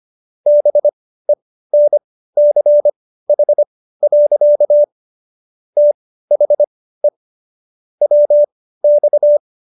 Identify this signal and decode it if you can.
Morse code, heard as a keyed tone, about 25 wpm, spaced as for 15 wpm, 590 Hz. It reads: BENCH. THE WX